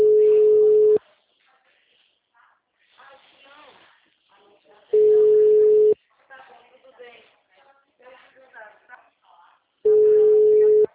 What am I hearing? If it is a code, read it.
Ringback tone (Europe, 425 Hz), with speech.